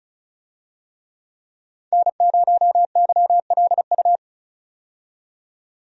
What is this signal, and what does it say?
Morse code, heard as a keyed tone, about 34 wpm, 700 Hz. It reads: N0YLU
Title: USA Call Signs - 5 Characters - 35wpm